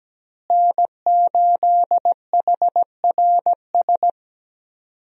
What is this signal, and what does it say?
Morse code, heard as a keyed tone, about 17 wpm, 700 Hz. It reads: N8HRS
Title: USA Call Signs - 5 Characters - Repeated 3x - 17wpm